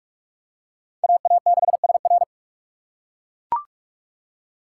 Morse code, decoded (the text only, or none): AA6SR